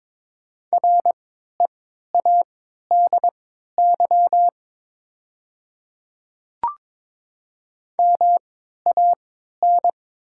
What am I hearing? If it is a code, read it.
Morse code, heard as a keyed tone, about 22 wpm, spaced as for 12 wpm, 700 Hz. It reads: READY MAN
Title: 3x Character Spacing - Top 500 Words - 22wpm